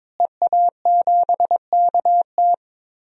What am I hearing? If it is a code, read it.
Morse code, heard as a keyed tone, about 22 wpm, 700 Hz. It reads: EA7KT
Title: DXCC-100 Call Signs - 5 Characters - Repeated 3x - 22wpm